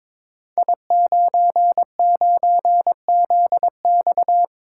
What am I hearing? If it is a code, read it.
Morse code, heard as a keyed tone, about 22 wpm, 700 Hz. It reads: I99ZX
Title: Code Groups - 5 Letter:Numbers - 22wpm